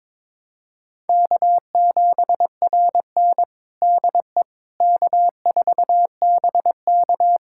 Morse code, heard as a keyed tone, about 22 wpm, 700 Hz. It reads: K7RN DE K4BK
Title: USA Callsign DE Callsign - 4 Characters - 22wpm